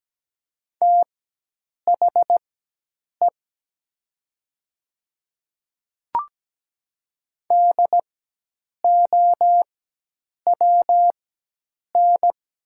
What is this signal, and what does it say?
Morse code, heard as a keyed tone, about 17 wpm, spaced as for 8 wpm, 700 Hz. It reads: THE DOWN
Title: 4x Character Spacing - Top 200 Words - 17wpm